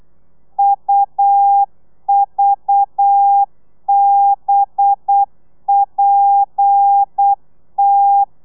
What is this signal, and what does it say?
Morse code, heard as a keyed tone, about 8 wpm, 800 Hz. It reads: UVBPT